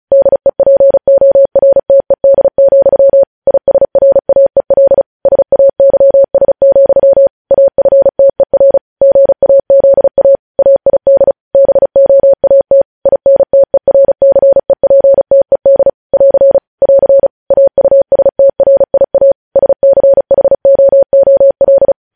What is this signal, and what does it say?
Morse code, heard as a keyed tone, about 36 wpm, 550 Hz. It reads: DEPORTED, ISRAEL SAYS, AFTER GAZA AID BOAT INTERCEPTED <AR> <AR> AUSTRIA SCHOOL
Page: news headlines in morse code at 35 wpm